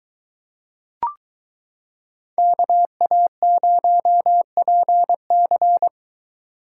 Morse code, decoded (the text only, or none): KA0PC